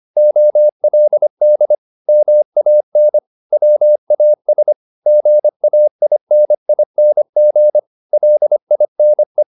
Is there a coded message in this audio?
Yes: Morse code from 0.2 to 9.4 s, heard as a keyed tone, about 25 wpm, 600 Hz. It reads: OLD MAN WAS GAINING LINE